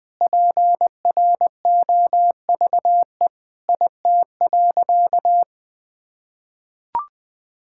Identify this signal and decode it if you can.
Morse code, heard as a keyed tone, about 21 wpm, 700 Hz. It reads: PROVE IT.
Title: Sentences with 8-Word Phrases - Spoken Beginning - 20wpm